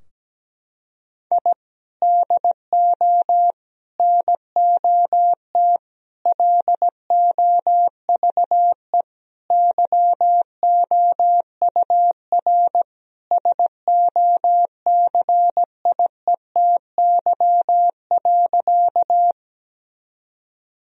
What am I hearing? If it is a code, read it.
Morse code, heard as a keyed tone, about 17 wpm, 700 Hz. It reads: I DO NOT LOVE YOUR SOCIETY.
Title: Sentences from Top 1000 Words - Review Part 4 - 17wpm